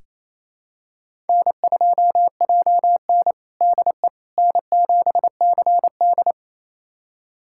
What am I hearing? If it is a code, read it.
Morse code, heard as a keyed tone, about 28 wpm, 700 Hz. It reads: N2JN DE N7CD
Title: USA Callsign DE Callsign - 4 Characters - 28wpm